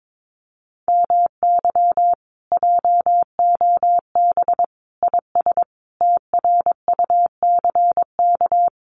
Morse code, heard as a keyed tone, about 22 wpm, 700 Hz. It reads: MY JOB IS TRUCK
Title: Common QSO Phrases Encore 22wpm